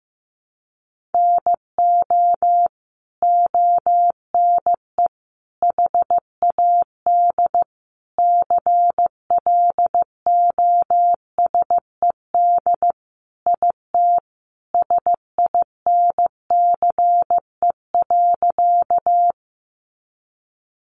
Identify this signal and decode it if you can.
Morse code, heard as a keyed tone, about 15 wpm, 700 Hz. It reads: NO ONE HAD CLOSED IT SINCE.
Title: Sentences from Top 1400 Words 15wpm